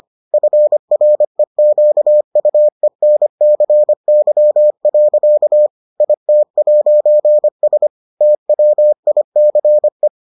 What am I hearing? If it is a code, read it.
Morse code, heard as a keyed tone, about 25 wpm, 600 Hz. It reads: FREQUENCY. IT'S TWICE